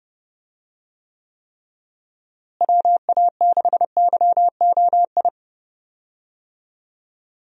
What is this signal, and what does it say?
Morse code, heard as a keyed tone, about 31 wpm, 700 Hz. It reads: WA6YOI
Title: USA Call Signs - 6 Characters - Repeated 3x - 30wpm